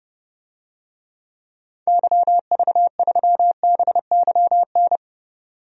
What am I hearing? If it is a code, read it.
Morse code, heard as a keyed tone, about 32 wpm, 700 Hz. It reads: YV3BYN
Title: DXCC-100 Call Signs - 6 Characters - 30wpm